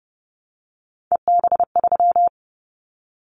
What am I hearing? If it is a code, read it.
Morse code, heard as a keyed tone, about 30 wpm, 700 Hz. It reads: EB3
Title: Callsign Prefixes for Top 25 DXCC - Repeated 3x - 30wpm